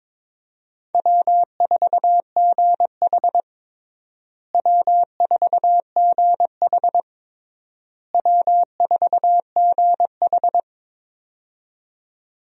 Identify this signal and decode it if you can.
Morse code, heard as a keyed tone, about 22 wpm, 700 Hz. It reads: W4GH W4GH W4GH
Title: USA Call Signs - 4 Characters - Repeated 3x - 22wpm